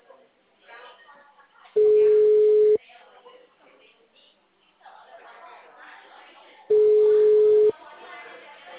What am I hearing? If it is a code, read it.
Ringback tone (Europe, 425 Hz), with speech.